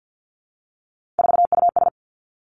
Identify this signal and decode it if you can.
Morse code, heard as a keyed tone, about 50 wpm, 700 Hz. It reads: 4US
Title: Three Challenging Letter-Number Combinations 50wpm